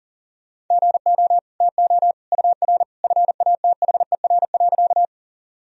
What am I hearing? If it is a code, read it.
Morse code, heard as a keyed tone, about 40 wpm, 700 Hz. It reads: GO TO UR FATHER.
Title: Sentences from Top 300 Words and Abbreviations 40wpm